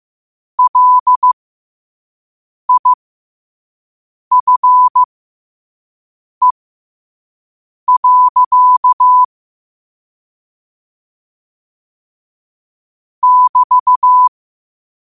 Morse code, heard as a keyed tone, about 15 wpm, spaced as for 5 wpm, 1000 Hz. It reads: LIFE. <BT>
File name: qotdfiles / 5wpm